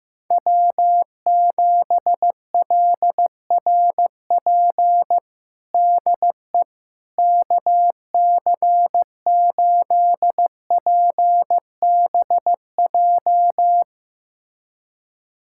Morse code, heard as a keyed tone, about 15 wpm, 700 Hz. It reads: W7LRP DE KC8PBJ
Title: USA Callsign DE Callsign 15wpm